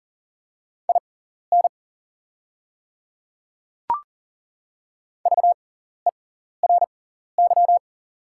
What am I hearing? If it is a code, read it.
Morse code, heard as a keyed tone, about 40 wpm, spaced as for 14 wpm, 690 Hz. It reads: IN VERY
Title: 6x Character Spacing - Top 300 Words - 40wpm